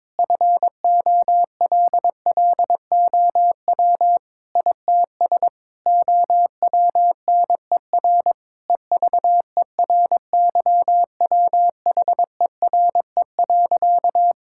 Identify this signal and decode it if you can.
Morse code, heard as a keyed tone, about 22 wpm, 700 Hz. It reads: FOLLOW ITS OWNER EVERYWHERE.
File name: Difficult Sentences with Hints - 22wpm